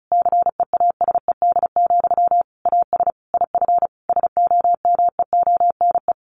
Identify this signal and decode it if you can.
Morse code, heard as a keyed tone, about 34 wpm, 700 Hz. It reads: CEASED, AS IF SOMEONE
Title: A Sound Of Thunder 35wpm